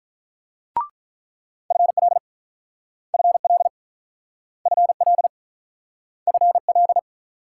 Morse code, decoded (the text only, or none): FL FL FL FL